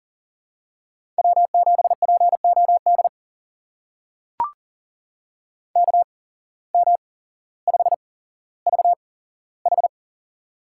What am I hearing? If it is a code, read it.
Morse code, heard as a keyed tone, about 40 wpm, spaced as for 17 wpm, 700 Hz. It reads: W7POD KM5VH
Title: USA Call Signs - 5 Characters - 8x Character Spacing - 40wpm